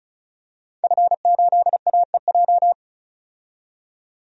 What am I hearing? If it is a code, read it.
Morse code, heard as a keyed tone, about 35 wpm, 700 Hz. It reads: F8AEJ
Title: DXCC-100 Call Signs - 5 Characters - 35wpm